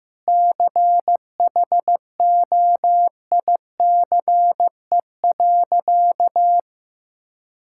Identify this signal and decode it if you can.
Morse code, heard as a keyed tone, about 15 wpm, 700 Hz. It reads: CHOICE.
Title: Sentences from Top 1000 Words - Review Part 6 - 15wpm